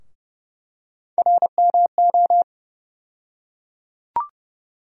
Morse code, heard as a keyed tone, about 30 wpm, 700 Hz. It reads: RMO
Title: USA Call Sign Postfixes - 3 Letters 30wpm